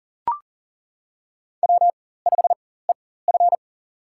Morse code, heard as a keyed tone, about 40 wpm, spaced as for 19 wpm, 700 Hz. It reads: W5EF
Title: USA Call Signs - 4 Characters - 4x Character Spacing - 40wpm